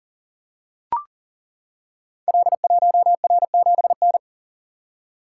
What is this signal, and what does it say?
Morse code, heard as a keyed tone, about 42 wpm, 700 Hz. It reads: L1R7N
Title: Code Groups - 5 Letter:Numbers - 40wpm